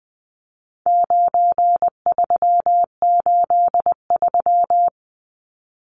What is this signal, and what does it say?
Morse code, heard as a keyed tone, about 20 wpm, 700 Hz. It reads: 9383
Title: Numbers - 4-Digit 20wpm